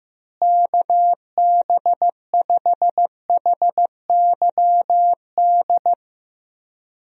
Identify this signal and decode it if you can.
Morse code, heard as a keyed tone, about 15 wpm, 700 Hz. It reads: KB5HYD
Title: USA Call Signs - 6 Characters - Repeated 2x - 15wpm